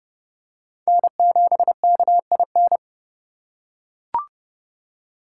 Morse code, heard as a keyed tone, about 29 wpm, 700 Hz. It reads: N7KIN